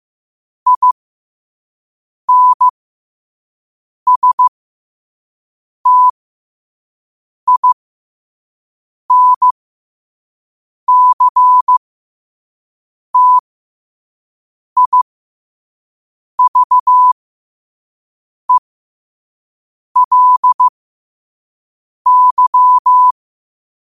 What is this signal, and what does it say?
Morse code, heard as a keyed tone, about 15 wpm, spaced as for 5 wpm, 1000 Hz. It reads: INSTINCTIVELY
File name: qotdfiles / 5wpm